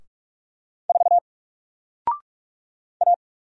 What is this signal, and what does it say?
Morse code, heard as a keyed tone, about 46 wpm, 700 Hz. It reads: 4 A
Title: Single Letter-Number Alt 45wpm